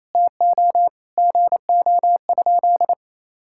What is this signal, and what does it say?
Morse code, heard as a keyed tone, about 28 wpm, 700 Hz. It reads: TO GO?
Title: Sentences from Top 1600 Words 28wpm